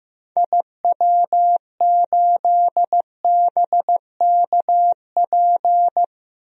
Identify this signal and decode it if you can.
Morse code, heard as a keyed tone, about 15 wpm, 700 Hz. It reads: IW8BKP
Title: DXCC-100 Call Signs - 6 Characters - 15wpm